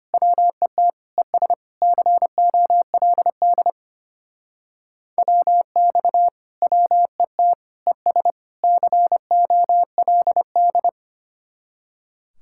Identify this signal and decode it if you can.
Morse code, heard as a keyed tone, about 33 wpm, 700 Hz. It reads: WET ES COLD WX WET ES COLD